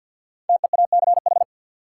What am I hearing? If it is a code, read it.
Morse code, heard as a keyed tone, about 52 wpm, 700 Hz. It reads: TEACH
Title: Sentences from Top 500 Words - New Word Drill - 50wpm